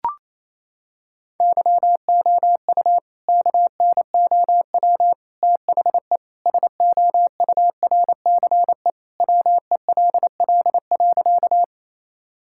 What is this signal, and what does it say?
Morse code, heard as a keyed tone, about 28 wpm, 700 Hz. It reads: YOU KNOW THE SOURCE WELL.
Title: Sentences from Top 1000 Words - Review Part 5 - 28wpm